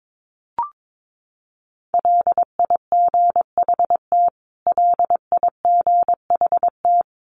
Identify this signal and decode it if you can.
Morse code, heard as a keyed tone, about 22 wpm, 700 Hz. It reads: LIGHT LIGHT
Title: Top 500 Words Repeated 22wpm